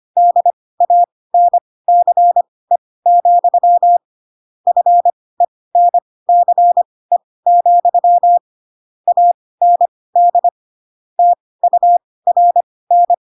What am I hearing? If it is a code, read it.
Morse code, heard as a keyed tone, about 25 wpm, spaced as for 18 wpm, 700 Hz. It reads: DANCE, FENCE, AND TURN
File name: Pinocchio / Pinocchio ch2 12 25 ik1hge